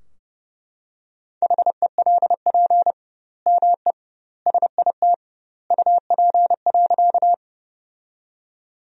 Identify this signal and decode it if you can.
Morse code, heard as a keyed tone, about 30 wpm, 700 Hz. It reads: HELP ME SIT UP.
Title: 2x Word-Spacing - Sentences from Top 500 Words - 30wpm